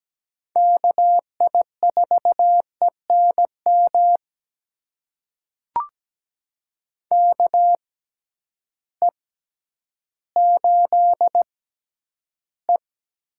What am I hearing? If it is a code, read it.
Morse code, heard as a keyed tone, about 17 wpm, spaced as for 10 wpm, 700 Hz. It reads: KI4ENM KE8E
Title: USA Call Signs - 6 Characters - 6x Character Spacing - 17wpm